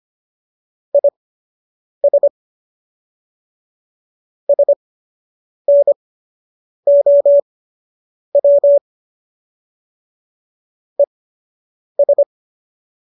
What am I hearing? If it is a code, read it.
Morse code, heard as a keyed tone, about 25 wpm, spaced as for 8 wpm, 570 Hz. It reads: IS SNOW ES